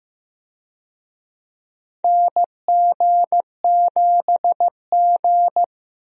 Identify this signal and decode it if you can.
Morse code, heard as a keyed tone, about 15 wpm, 700 Hz. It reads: NG7G
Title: USA Call Signs - 4 Characters - 15wpm